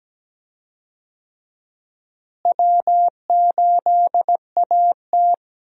Morse code, heard as a keyed tone, about 17 wpm, 700 Hz. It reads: W8AT
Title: USA Call Signs - 4 Characters - Repeated 2x - 17wpm